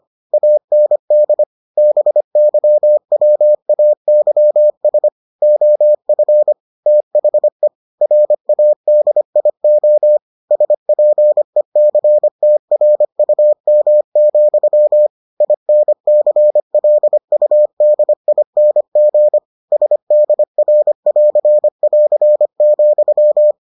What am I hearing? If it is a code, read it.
Morse code, heard as a keyed tone, about 25 wpm, 600 Hz. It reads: AND BYWAYS OF THE RADIO SPECTRUM, INCLUDING SDR<AR><AR>,